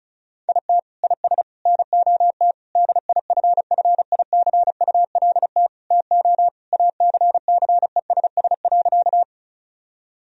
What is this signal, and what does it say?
Morse code, heard as a keyed tone, about 34 wpm, 700 Hz. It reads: IT IS NOT DIFFICULT TO ACCESS.
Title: Sentences from Top 1000 Words - Review Part 4 - 35wpm